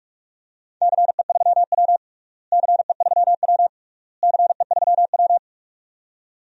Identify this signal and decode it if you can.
Morse code, heard as a keyed tone, about 46 wpm, 700 Hz. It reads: CE3W CE3W CE3W
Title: DXCC-100 Call Signs - 4 Characters - Repeated 3x - 45wpm